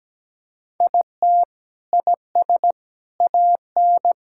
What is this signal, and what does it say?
Morse code, heard as a keyed tone, about 17 wpm, 700 Hz. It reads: IT IS AN
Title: Sentences from Top 1000 Words - Review Part 6 - 17wpm